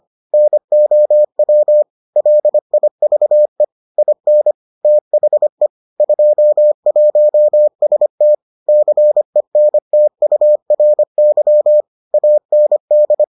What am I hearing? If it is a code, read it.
Morse code, heard as a keyed tone, about 25 wpm, 600 Hz. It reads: NOW LIVE IN THE 21ST CENTURY AND